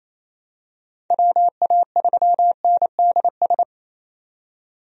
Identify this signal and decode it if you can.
Morse code, heard as a keyed tone, about 27 wpm, 700 Hz. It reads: WA3NDS
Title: USA Call Signs - 6 Characters - 28wpm